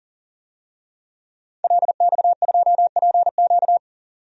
Morse code, heard as a keyed tone, about 40 wpm, 700 Hz. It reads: LX2PQ
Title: DXCC-100 Call Signs - 5 Characters - 40wpm